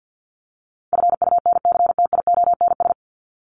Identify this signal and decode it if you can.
Morse code, heard as a keyed tone, about 48 wpm, 700 Hz. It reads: FUNCTIONS
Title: ES:S Suffix - Top 100 Words with Base Words 50wpm